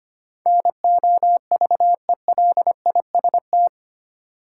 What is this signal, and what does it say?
Morse code, heard as a keyed tone, about 25 wpm, 700 Hz. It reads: NOVELIST
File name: Eight Letter Words - 25wpm